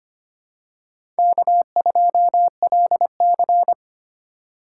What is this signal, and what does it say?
Morse code, heard as a keyed tone, about 25 wpm, 700 Hz. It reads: K2LC
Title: USA Call Signs - 4 Characters - 4x Character Spacing - 25wpm